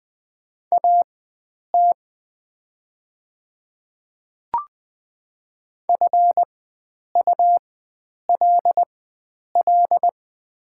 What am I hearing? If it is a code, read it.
Morse code, heard as a keyed tone, about 20 wpm, spaced as for 9 wpm, 700 Hz. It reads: AT FULL
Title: 4x Character Spacing - Top 500 Words - 20wpm